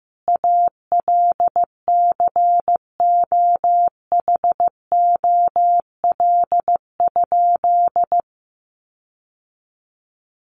Morse code, heard as a keyed tone, about 15 wpm, 700 Hz. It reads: ALCOHOL?
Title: Sentences from Top 1400 Words 15wpm